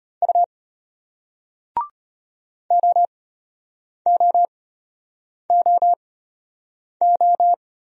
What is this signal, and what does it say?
Morse code, heard as a keyed tone, about 34 wpm, spaced as for 8 wpm, 700 Hz. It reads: U OOOO